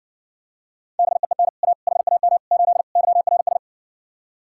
Morse code, heard as a keyed tone, about 61 wpm, 700 Hz. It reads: BEEN A HAM 8 YRS